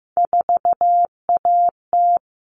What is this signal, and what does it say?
Morse code, heard as a keyed tone, about 15 wpm, 700 Hz. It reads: 4AT
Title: All Cut Numbers - 3-Digit 15wpm